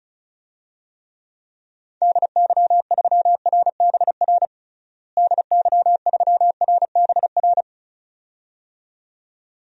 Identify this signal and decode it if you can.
Morse code, heard as a keyed tone, about 36 wpm, 700 Hz. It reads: DY3RBR DY3RBR
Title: DXCC-100 Call Signs - 6 Characters - Repeated 2x - 35wpm